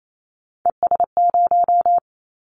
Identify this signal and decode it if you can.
Morse code, heard as a keyed tone, about 28 wpm, 700 Hz. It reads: ES0
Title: Callsign Prefixes for Top 100 DXCC - 28wpm